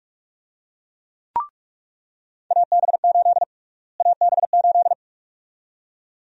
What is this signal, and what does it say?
Morse code, heard as a keyed tone, about 46 wpm, 700 Hz. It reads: AB8 AB8
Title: USA Call Sign Prefixes - Repeated 2x - 45wpm